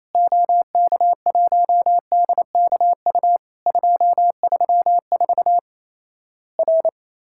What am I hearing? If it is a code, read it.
Morse code, heard as a keyed tone, about 28 wpm, 700 Hz. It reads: OK1DKU 234 R
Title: Worked All Europe DX Contest - QTC Exchange - 28wpm